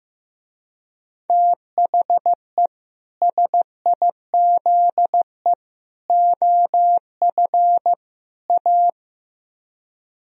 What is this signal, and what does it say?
Morse code, heard as a keyed tone, about 15 wpm, 700 Hz. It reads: THE SIZE OF A
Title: Speed-Racing - Sets of 4 Words 15wpm